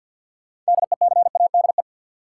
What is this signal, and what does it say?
Morse code, heard as a keyed tone, about 52 wpm, 700 Hz. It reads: DECADE
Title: Sets of 5 Words - New Word Drill - 50wpm